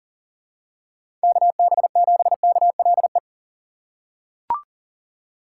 Morse code, heard as a keyed tone, about 40 wpm, 700 Hz. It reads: KB7KLE
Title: USA Call Signs - 6 Characters - 2x Character Spacing - 40wpm